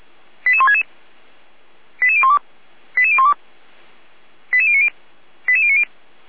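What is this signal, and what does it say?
Five-tone selective call (ZVEI), heard as quick runs of five tones, tones of about 70 ms. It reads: 88166 88012 88012 88090 88090
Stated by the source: ZVEI five-tone call received by SDR, served by Feuerwehr Orlamünde.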